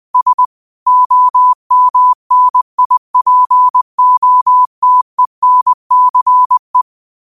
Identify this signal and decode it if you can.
Morse code, heard as a keyed tone, about 20 wpm, 1000 Hz. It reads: S OMNIPOTENCE
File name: qotdfiles / 20wpm